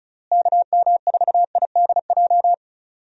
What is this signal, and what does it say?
Morse code, heard as a keyed tone, about 34 wpm, 700 Hz. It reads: KM4IDJ
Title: USA Call Signs - 6 Characters - Repeated 3x - 35wpm